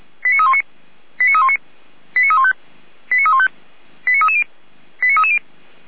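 Five-tone selective call (ZVEI), heard as quick runs of five tones, tones of about 70 ms. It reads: 89319 89319 89315 89315 89330 89330
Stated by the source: ZVEI five-tone selective call received on SDR.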